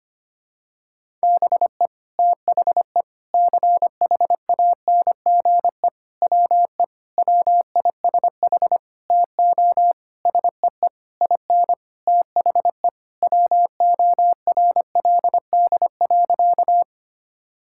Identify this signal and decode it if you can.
Morse code, heard as a keyed tone, about 25 wpm, 700 Hz. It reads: BE THE CHANGE WE WISH TO SEE IN THE WORLD.
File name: practice / 25wpm / Sentences with 9-Word Phrases - 25wpm